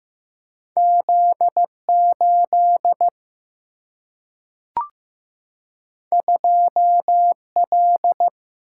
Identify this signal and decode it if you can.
Morse code, heard as a keyed tone, about 15 wpm, 700 Hz. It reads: Z8 2L
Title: Single-Letter Single-Number 15wpm